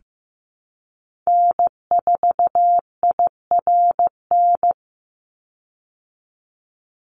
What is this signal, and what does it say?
Morse code, heard as a keyed tone, about 15 wpm, 700 Hz. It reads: N4IRN
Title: Call Signs Repeated 3x - 15wpm